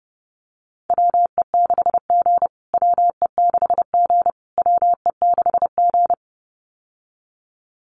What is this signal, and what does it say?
Morse code, heard as a keyed tone, about 30 wpm, 700 Hz. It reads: WE6G WE6G WE6G
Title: USA Call Signs Repeated 3x 30wpm